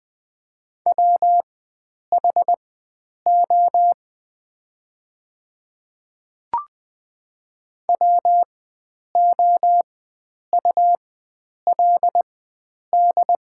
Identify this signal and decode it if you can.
Morse code, heard as a keyed tone, about 20 wpm, spaced as for 9 wpm, 700 Hz. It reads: WHO WOULD
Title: 4x Character Spacing - Top 100 Words - 20wpm